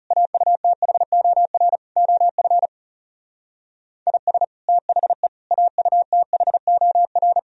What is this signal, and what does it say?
Morse code, heard as a keyed tone, about 39 wpm, 700 Hz. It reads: AUTHOR OF IS THE AUTHOR